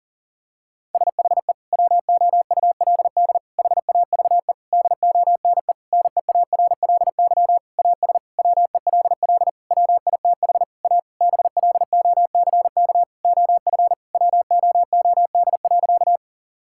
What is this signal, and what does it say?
Morse code, heard as a keyed tone, about 40 wpm, 700 Hz. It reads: SHE WOULD HAVE DONE NEARLY AS WELL WITH A BLOCK OF WOOD.
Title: Sentences from Top 1600 Words 40wpm